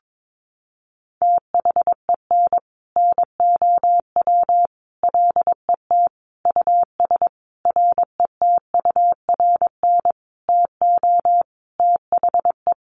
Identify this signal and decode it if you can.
Morse code, heard as a keyed tone, about 22 wpm, 700 Hz. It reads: THEN NOW LET US RETURN TO THE
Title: Sentences from Top 1500 Words 22wpm